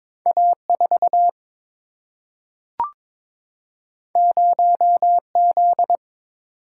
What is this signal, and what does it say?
Morse code, heard as a keyed tone, about 22 wpm, 700 Hz. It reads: A 4 0 Z
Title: Single-Letter Single-Number 22wpm